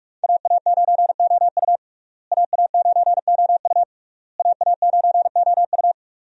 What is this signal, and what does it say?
Morse code, heard as a keyed tone, about 46 wpm, 700 Hz. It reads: AA9OU AA9OU AA9OU
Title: USA Call Signs - 5 Characters - Repeated 3x - 45wpm